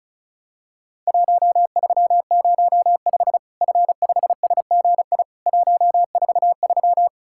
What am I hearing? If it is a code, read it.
Morse code, heard as a keyed tone, about 34 wpm, 700 Hz. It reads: 1305 F5SGI 143
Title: Worked All Europe DX Contest - QTC Exchange - 35wpm